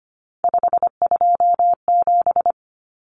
Morse code, heard as a keyed tone, about 25 wpm, 700 Hz. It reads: 527
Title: Numbers - 3-Digit 25wpm